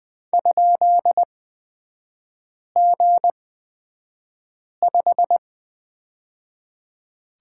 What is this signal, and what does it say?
Morse code, heard as a keyed tone, about 20 wpm, spaced as for 5 wpm, 700 Hz. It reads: ?G5
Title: 236 Lesson 38e - Letter q - Three Characters 20wpm